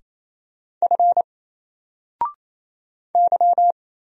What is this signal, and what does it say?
Morse code, heard as a keyed tone, about 28 wpm, 700 Hz. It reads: F Y
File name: Single-Letter-Number-28wpm